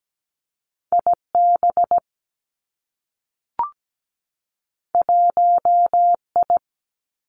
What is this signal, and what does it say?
Morse code, heard as a keyed tone, about 17 wpm, 700 Hz. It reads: IB 1I
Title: Two Challenging Letter-Number Combinations 17wpm